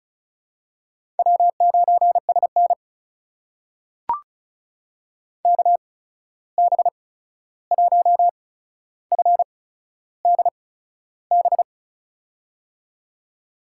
Morse code, heard as a keyed tone, about 36 wpm, spaced as for 13 wpm, 700 Hz. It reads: W9SN KB1FDB
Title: USA Call Signs - 8x Character Spacing - 35wpm